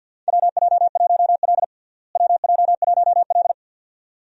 Morse code, heard as a keyed tone, about 46 wpm, 700 Hz. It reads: WJ1L WJ1L
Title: USA Call Signs - 4 Characters - Repeated 2x - 50wpm